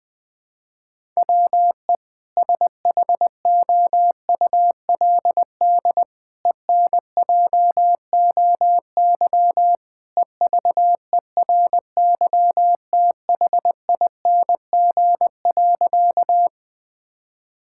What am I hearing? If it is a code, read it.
Morse code, heard as a keyed tone, about 21 wpm, 700 Hz. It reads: WE SHOULD ENJOY EVERYTHING.
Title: Sentences from Top 1000 Words - Review Part 5 - 20wpm